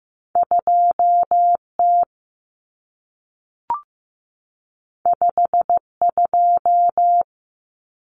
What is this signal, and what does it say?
Morse code, heard as a keyed tone, about 15 wpm, 700 Hz. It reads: 2T 52
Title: Common Cut Numbers - 2-Digit 15wpm